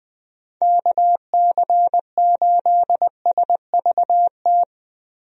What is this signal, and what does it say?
Morse code, heard as a keyed tone, about 21 wpm, 700 Hz. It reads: KC8SVT
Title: USA Call Signs - 6 Characters - Repeated 3x - 20wpm